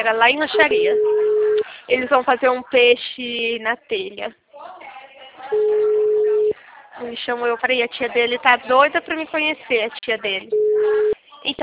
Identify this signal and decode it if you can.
Ringback tone (Europe, 425 Hz), with speech.